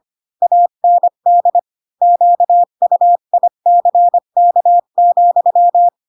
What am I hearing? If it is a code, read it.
Morse code, heard as a keyed tone, about 25 wpm, 700 Hz. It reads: AND QUICK,
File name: Pinocchio / Pinocchio ch8 20 25 ik1hge